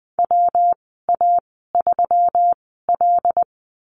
Morse code, heard as a keyed tone, about 20 wpm, spaced as for 15 wpm, 700 Hz. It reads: WA3L
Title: 154 Lesson 25g - Letter p - Call Signs 20wpm